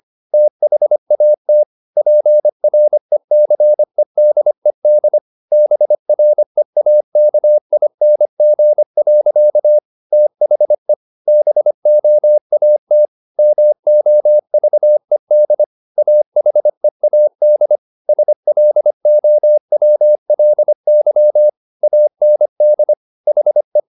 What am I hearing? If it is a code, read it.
Morse code, heard as a keyed tone, about 25 wpm, 600 Hz. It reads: THAT PRECEDED BREAKING. THE BOAT MOVED AHEAD SLOWLY AND HE